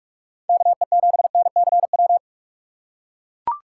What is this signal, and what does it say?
Morse code, heard as a keyed tone, about 44 wpm, 700 Hz. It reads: KE7NCW
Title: USA Call Signs - 6 Characters - 6x Character Spacing - 45wpm